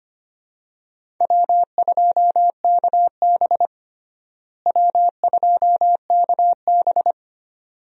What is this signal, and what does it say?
Morse code, heard as a keyed tone, about 25 wpm, 700 Hz. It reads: W2KB W2KB
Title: USA Call Signs - 4 Characters - Repeated 2x - 25wpm